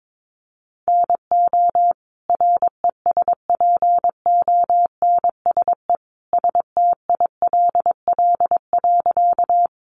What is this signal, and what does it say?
Morse code, heard as a keyed tone, about 22 wpm, 700 Hz. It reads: NO RESPONSE STILL.
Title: Sentences from Top 1400 Words 22wpm